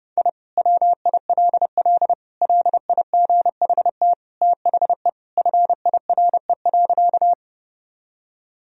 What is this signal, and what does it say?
Morse code, heard as a keyed tone, about 30 wpm, 700 Hz. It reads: I WILL LIGHT THE FIRE.
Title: Sentences from Top 1000 Words - Review Part 1 - 30wpm